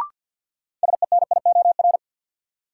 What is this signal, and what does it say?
Morse code, heard as a keyed tone, about 51 wpm, 700 Hz. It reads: SENIOR
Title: Sets of 5 Words Encore - New Word Drill - 50wpm